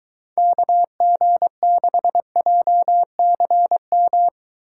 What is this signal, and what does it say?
Morse code, heard as a keyed tone, about 22 wpm, 700 Hz. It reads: KG6JCM